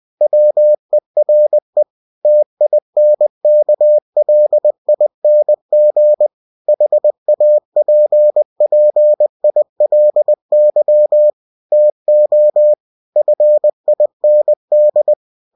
Morse code, heard as a keyed tone, about 20 wpm, 600 Hz. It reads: WERE TINKLING HAPPILY TO FIND
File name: cw-books / winnie-the-pooh / winnie-20wpm-page-0101